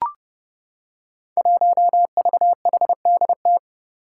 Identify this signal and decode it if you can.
Morse code, heard as a keyed tone, about 30 wpm, 700 Hz. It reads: 1VHDT
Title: Five Challenging Letter-Number Combinations 30wpm